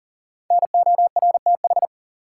Morse code, heard as a keyed tone, about 40 wpm, 700 Hz. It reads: NORTH
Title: Sets of 5 Words Encore - New Word Drill - 40wpm